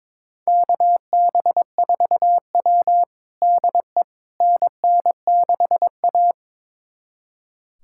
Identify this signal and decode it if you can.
Morse code, heard as a keyed tone, about 22 wpm, 700 Hz. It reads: KB4W DE NN6A
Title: USA Callsign DE Callsign - 4 Characters - 22wpm